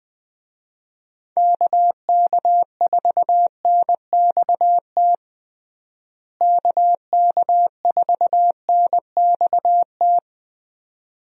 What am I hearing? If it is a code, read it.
Morse code, heard as a keyed tone, about 20 wpm, 700 Hz. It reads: KK4NXT KK4NXT
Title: USA Call Signs - 6 Characters - Repeated 2x - 20wpm